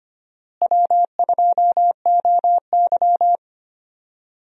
Morse code, heard as a keyed tone, about 25 wpm, 700 Hz. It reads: W2OY
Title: USA Call Signs - 4 Characters - Repeated 3x - 25wpm